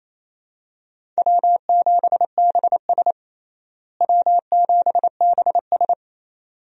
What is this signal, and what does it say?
Morse code, heard as a keyed tone, about 28 wpm, 700 Hz. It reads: W7BS W7BS
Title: USA Call Signs - 4 Characters - Repeated 2x - 28wpm